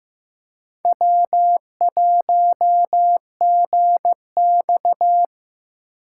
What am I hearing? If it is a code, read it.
Morse code, heard as a keyed tone, about 15 wpm, 700 Hz. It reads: W1GX
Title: USA Call Signs - 4 Characters - 6x Character Spacing - 15wpm